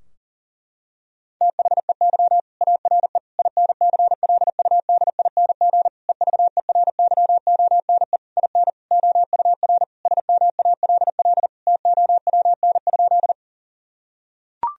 Morse code, heard as a keyed tone, about 40 wpm, 700 Hz. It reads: THEY ARE INCLUDING EVERYONE IN OUR SMALL TOWN?
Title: Sentences from Top 1000 Words - Review Part 4 - 40wpm